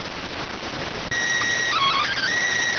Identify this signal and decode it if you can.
SSTV (slow-scan television), PD 290.